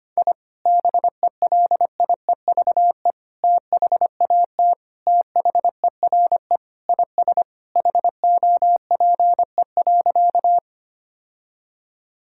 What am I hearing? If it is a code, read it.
Morse code, heard as a keyed tone, about 25 wpm, 700 Hz. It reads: I BELIEVE THAT THERE IS HOPE.
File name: practice / 25wpm / Sentences with 3-Word Phrases - 25wpm